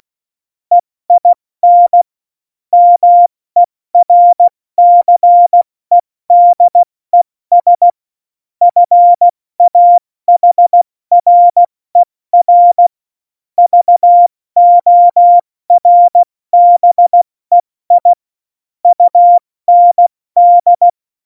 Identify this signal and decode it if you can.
Morse code, heard as a keyed tone, about 16 wpm, 700 Hz. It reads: EIN MERCEDES FAHRER VORBEI UND
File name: Morsekurs / Witze / Witz 05. Februar 2019, 70 F80